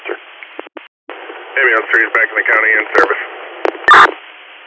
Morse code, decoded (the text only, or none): none